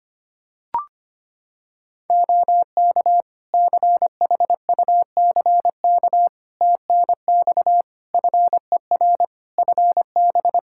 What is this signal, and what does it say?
Morse code, heard as a keyed tone, about 25 wpm, 700 Hz. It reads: OK CHUCK TNX FER FB
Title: Practice QSO - Exchange 4 - Ending - 25wpm